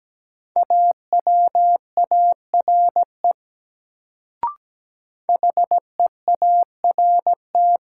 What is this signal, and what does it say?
Morse code, heard as a keyed tone, about 17 wpm, 700 Hz. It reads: AWARE HEART
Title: Sentences from Top 1200 Words - New Word Drill Alt - 17wpm